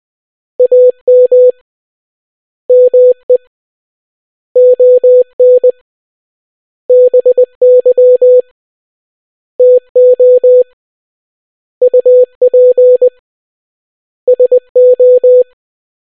Morse code, heard as a keyed tone, about 20 wpm, 500 Hz. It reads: AM ME ON BY TO UP SO